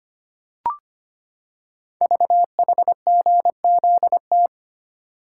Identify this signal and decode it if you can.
Morse code, heard as a keyed tone, about 25 wpm, 700 Hz. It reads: VHGZT
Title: Code Groups - 5 Letters - 25wpm